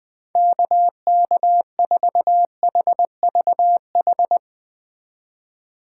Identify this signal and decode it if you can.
Morse code, heard as a keyed tone, about 20 wpm, 700 Hz. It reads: KK4HVH
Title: USA Call Signs - 6 Characters - 2x Character Spacing - 20wpm